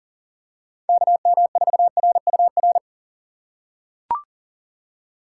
Morse code, heard as a keyed tone, about 40 wpm, 700 Hz. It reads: KM4RUR
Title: USA Call Signs - 6 Characters - 8x Character Spacing - 40wpm